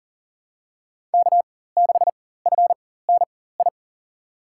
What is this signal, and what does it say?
Morse code, heard as a keyed tone, about 39 wpm, spaced as for 19 wpm, 700 Hz. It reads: K6FNI
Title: USA Call Signs - 5 Characters - 4x Character Spacing - 40wpm